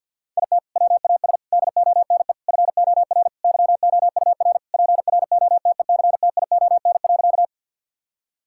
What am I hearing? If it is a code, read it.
Morse code, heard as a keyed tone, about 50 wpm, 700 Hz. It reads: IT WAS DONE FOR YOUR PROTECTION.
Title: Sentences from Top 1600 Words 50wpm